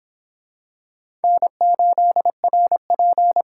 Morse code, heard as a keyed tone, about 22 wpm, 700 Hz. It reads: N8RP